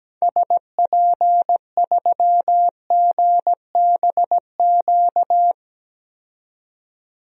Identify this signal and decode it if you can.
Morse code, heard as a keyed tone, about 17 wpm, 700 Hz. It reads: SP3GBQ
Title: DXCC-100 Call Signs - 6 Characters - 17wpm